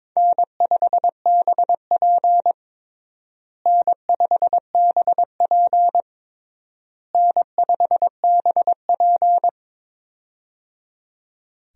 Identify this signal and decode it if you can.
Morse code, heard as a keyed tone, about 22 wpm, 700 Hz. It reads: N5BP N5BP N5BP
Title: USA Call Signs - 4 Characters - Repeated 3x - 22wpm